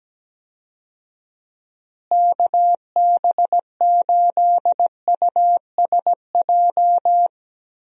Morse code, heard as a keyed tone, about 17 wpm, 700 Hz. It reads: KB8USJ
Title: USA Call Signs - 6 Characters - Repeated 3x - 17wpm